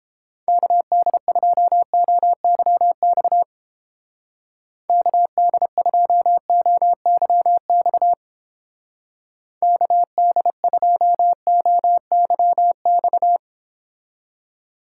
Morse code, heard as a keyed tone, about 29 wpm, 700 Hz. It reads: KD2OYX KD2OYX KD2OYX